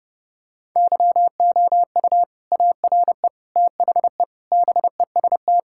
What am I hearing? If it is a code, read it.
Morse code, heard as a keyed tone, about 30 wpm, 700 Hz. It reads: YOU ARE THE BEST
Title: Sentences from Top 1000 Words - Review Part 6 - 30wpm